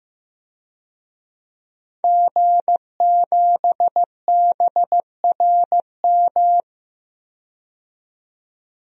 Morse code, heard as a keyed tone, about 15 wpm, 700 Hz. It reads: G7BRM
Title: DXCC-100 Call Signs - 5 Characters - 15wpm